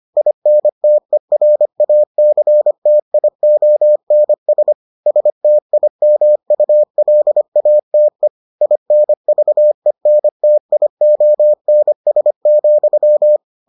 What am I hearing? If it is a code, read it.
Morse code, heard as a keyed tone, about 25 wpm, 600 Hz. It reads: INTERACTIONS STIMULATE INVENTIONS,